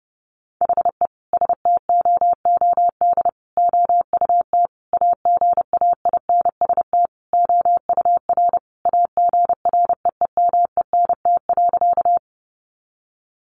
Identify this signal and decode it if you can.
Morse code, heard as a keyed tone, about 30 wpm, 700 Hz. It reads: HE STOOD OUT AGAINST OUR AGREEMENT.
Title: Sentences from Top 1500 Words 30wpm